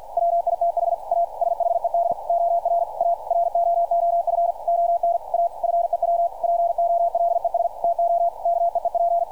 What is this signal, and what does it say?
Morse code, heard as a keyed tone, about 52 wpm, 690 Hz, with radio noise. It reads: WITH A SHED OR A ROOF ON A FEW POLES TO KEEP
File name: archived / CRSnet